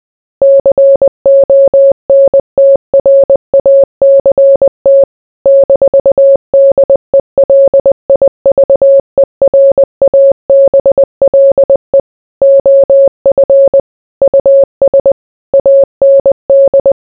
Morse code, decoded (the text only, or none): CONTRACT -DELIVERABLE OF US AND